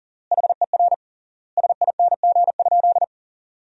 Morse code, heard as a keyed tone, about 40 wpm, 700 Hz. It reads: HER SING?
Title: 3x Word-Spacing - Sentences from Top 400 Words - 40wpm